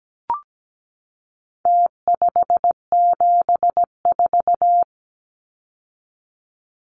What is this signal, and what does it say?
Morse code, heard as a keyed tone, about 17 wpm, 700 Hz. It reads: T574
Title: Numbers - Contest Style - 4-Digit 17wpm